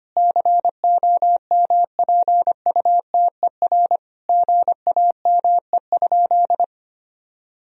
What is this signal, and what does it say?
Morse code, heard as a keyed tone, about 25 wpm, 700 Hz. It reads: COMPUTER GAME?
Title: Sentences from Top 1000 Words - Review Part 3 - 25wpm